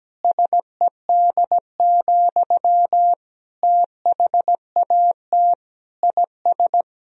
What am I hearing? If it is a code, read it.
Morse code, heard as a keyed tone, about 17 wpm, 700 Hz. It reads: SED, THAT IS
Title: Sentences from Top 500 Words and Abbreviations 17wpm